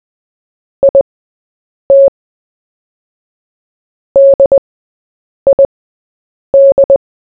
Morse code, heard as a keyed tone, about 20 wpm, spaced as for 8 wpm, 550 Hz. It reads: IT DID
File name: Short-Story-4-8wpm